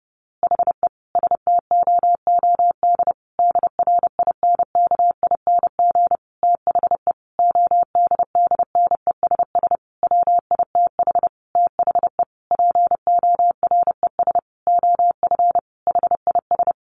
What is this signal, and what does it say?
Morse code, heard as a keyed tone, about 30 wpm, 700 Hz. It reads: HE STOOD DRINKING THE ODDNESS WITH THE PORES OF HIS
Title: A Sound Of Thunder 30wpm